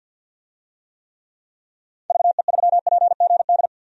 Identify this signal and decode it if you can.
Morse code, heard as a keyed tone, about 48 wpm, 700 Hz. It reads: VE3PGD
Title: DXCC-100 Call Signs - 6 Characters - 50wpm